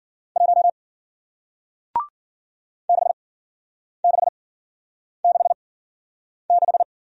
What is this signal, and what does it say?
Morse code, heard as a keyed tone, about 53 wpm, spaced as for 9 wpm, 700 Hz. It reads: 1 6666